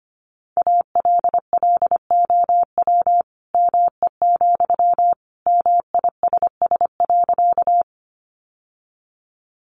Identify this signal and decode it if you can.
Morse code, heard as a keyed tone, about 25 wpm, 700 Hz. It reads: ALLOW ME, MISS.
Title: Sentences from Top 700 Words 25wpm